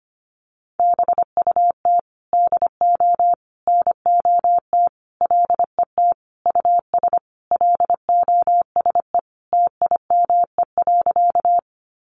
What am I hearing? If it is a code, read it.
Morse code, heard as a keyed tone, about 25 wpm, 700 Hz. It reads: BUT DO NOT LET US LOSE TIME.
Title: Sentences from Top 1100 Words 25wpm